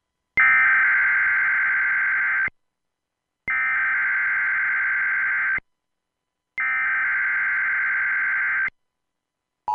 Emergency Alert System (SAME) tones (header bursts), heard as harsh buzzing bursts. It is heard from 0.4 to 8.7 s.